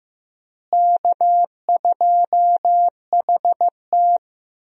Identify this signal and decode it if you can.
Morse code, heard as a keyed tone, about 15 wpm, 700 Hz. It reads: K2HT
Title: USA Call Signs - 4 Characters - Repeated 3x - 15wpm